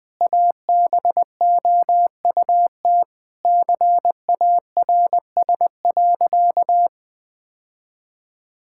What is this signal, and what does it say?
Morse code, heard as a keyed tone, about 20 wpm, 700 Hz. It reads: ABOUT CARS.
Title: Sentences with 5-Word Phrases - 20wpm